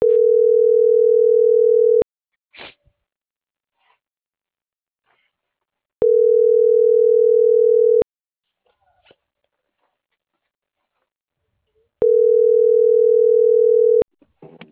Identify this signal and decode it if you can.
Ringback tone (North America), 440 + 480 Hz.